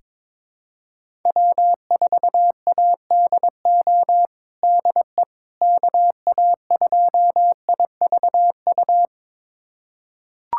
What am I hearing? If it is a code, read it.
Morse code, heard as a keyed tone, about 22 wpm, 700 Hz. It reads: W4ADO DE KA2IVU
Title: USA Callsign DE Callsign 22wpm